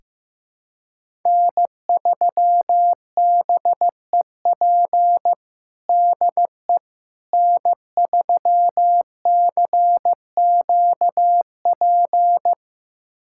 Morse code, heard as a keyed tone, about 15 wpm, 700 Hz. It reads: N3BEP DE N3CQP
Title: USA Callsign DE Callsign - 5 Characters - 15wpm